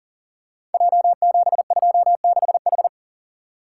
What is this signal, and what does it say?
Morse code, heard as a keyed tone, about 39 wpm, 700 Hz. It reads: J726H
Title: Code Groups - 5 Letter:Numbers - 40wpm